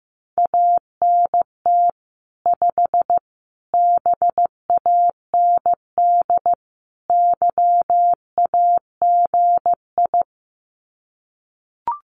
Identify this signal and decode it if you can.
Morse code, heard as a keyed tone, about 15 wpm, 700 Hz. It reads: ANT 5 BAND YAGI
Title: Common QSO Phrases 15wpm